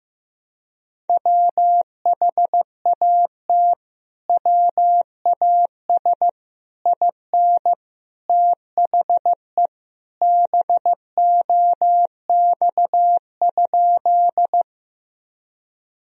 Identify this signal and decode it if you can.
Morse code, heard as a keyed tone, about 15 wpm, 700 Hz. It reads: WHAT WAS IN THE BOX?
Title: Sentences from Top 1000 Words - Review Part 2 - 15wpm